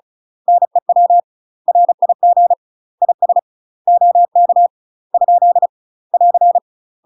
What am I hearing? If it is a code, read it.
Morse code, heard as a keyed tone, about 34 wpm, 700 Hz. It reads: NEW RIG IS OK ? <AR>
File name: UK MASTER QSOs TRAINING / 35WPM / UK MASTER QSOs 0039